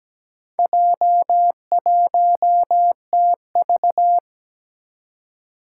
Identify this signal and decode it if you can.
Morse code, heard as a keyed tone, about 17 wpm, 700 Hz. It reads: J1TV
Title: Four Challenging Letter-Number Combinations 17wpm